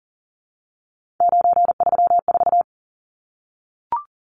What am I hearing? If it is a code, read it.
Morse code, heard as a keyed tone, about 40 wpm, 700 Hz. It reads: 934
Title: Numbers - 3-Digit 40wpm